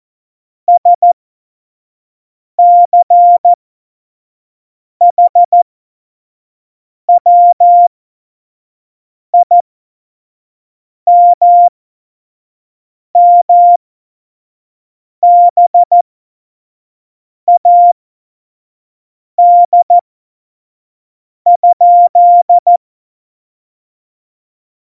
Morse code, heard as a keyed tone, about 14 wpm, spaced as for 5 wpm, 700 Hz. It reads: SCHWIMMBAD?
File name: Morsekurs / Witze / Witz 12. Januar 2021, 25 F70